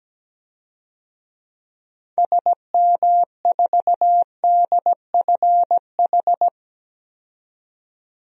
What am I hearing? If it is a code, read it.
Morse code, heard as a keyed tone, about 17 wpm, 700 Hz. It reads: SM4DFH
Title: DXCC-100 Call Signs - 6 Characters - 17wpm